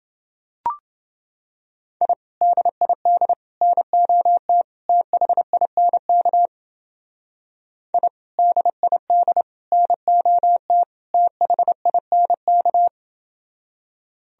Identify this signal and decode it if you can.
Morse code, heard as a keyed tone, about 30 wpm, 700 Hz. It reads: I DID NOT THINK I DID NOT THINK